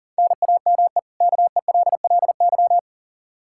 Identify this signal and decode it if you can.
Morse code, heard as a keyed tone, about 40 wpm, 700 Hz. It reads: NAME KELLY
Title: Practice QSO - Exchange 1 - Beginning - 40wpm